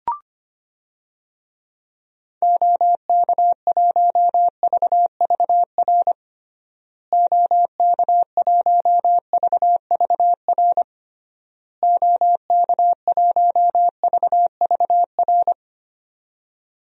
Morse code, heard as a keyed tone, about 25 wpm, 700 Hz. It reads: OK1VVR OK1VVR OK1VVR
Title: DXCC-100 Call Signs - 6 Characters - Repeated 3x - 25wpm